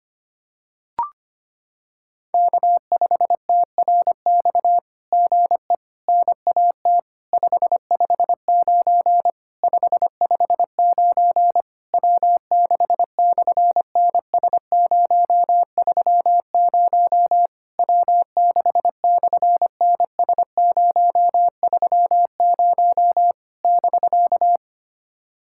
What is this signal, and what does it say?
Morse code, heard as a keyed tone, about 25 wpm, 700 Hz. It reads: K5TRX GE NAT 559 559 W6/NS030 W6/NS030 <BK>
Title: SOTA QSO Practice 25wpm